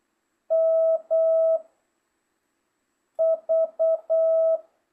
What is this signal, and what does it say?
Morse code, heard as a keyed tone, about 8 wpm, 1300 Hz, with noise. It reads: M V